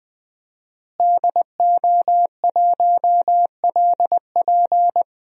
Morse code, heard as a keyed tone, about 20 wpm, 700 Hz. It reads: DO1LP
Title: DXCC-100 Call Signs - 5 Characters - Repeated 2x - 20wpm